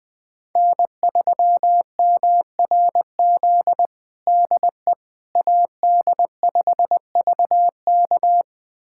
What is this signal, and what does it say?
Morse code, heard as a keyed tone, about 20 wpm, 700 Hz. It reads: N3MRZ DE AD5VK
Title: USA Callsign DE Callsign - 5 Characters - 20wpm